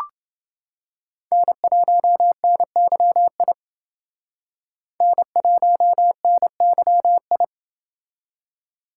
Morse code, heard as a keyed tone, about 29 wpm, 700 Hz. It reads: N1NYI N1NYI